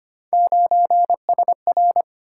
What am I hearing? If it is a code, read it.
Morse code, heard as a keyed tone, about 25 wpm, 700 Hz. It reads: 9SR
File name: Three Letter-Number Combinations - 25wpm